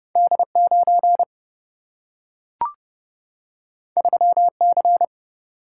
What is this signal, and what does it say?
Morse code, heard as a keyed tone, about 30 wpm, 700 Hz. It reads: D 9 3 C
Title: Single-Letter Single-Number 30wpm